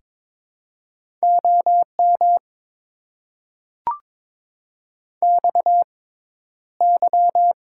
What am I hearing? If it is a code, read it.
Morse code, heard as a keyed tone, about 22 wpm, 700 Hz. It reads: O M X Y
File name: Top 90 QSO Elements - 6x Character Spacing - 22wpm